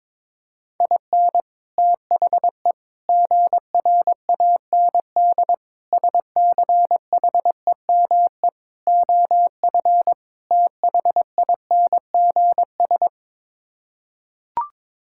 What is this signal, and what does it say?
Morse code, heard as a keyed tone, about 22 wpm, 700 Hz. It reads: IN THE GRAND SCHEME OF THINGS
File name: practice / 22wpm / Sets of 6 Words - 22wpm